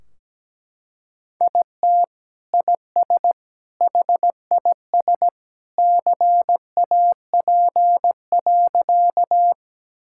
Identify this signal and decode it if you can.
Morse code, heard as a keyed tone, about 17 wpm, 700 Hz. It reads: IT IS HIS CAP.
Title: Sentences from Top 1600 Words 17wpm